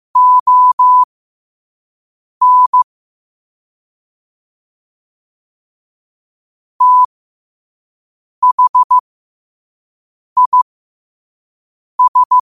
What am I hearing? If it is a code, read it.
Morse code, heard as a keyed tone, about 15 wpm, spaced as for 5 wpm, 1000 Hz. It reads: ON THIS
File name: qotdfiles / 5wpm